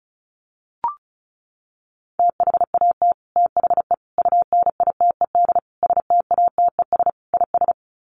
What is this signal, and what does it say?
Morse code, heard as a keyed tone, about 34 wpm, 700 Hz. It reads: THAT THE UNITED STATES IS
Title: Sets of 5 Words Encore 35wpm